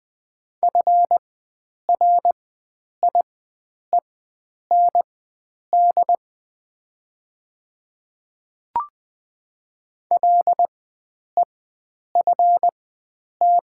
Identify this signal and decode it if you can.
Morse code, heard as a keyed tone, about 20 wpm, spaced as for 9 wpm, 700 Hz. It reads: FRIEND LEFT
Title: 4x Character Spacing - Top 300 Words - 20wpm